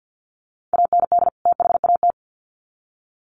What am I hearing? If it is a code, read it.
Morse code, heard as a keyed tone, about 50 wpm, 700 Hz. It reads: AND THAT
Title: Sets of 2 Words 50wpm